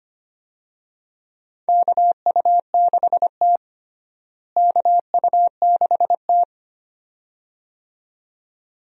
Morse code, heard as a keyed tone, about 25 wpm, 700 Hz. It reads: KU6T KU6T
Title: USA Call Signs - 4 Characters - Repeated 2x - 25wpm